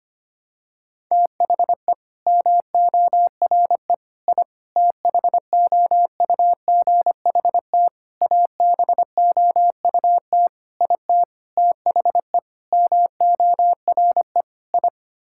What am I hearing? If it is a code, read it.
Morse code, heard as a keyed tone, about 25 wpm, 700 Hz. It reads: THE MORE I THOUGHT ABOUT IT THE MORE I
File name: Sets of 9 Words - 25wpm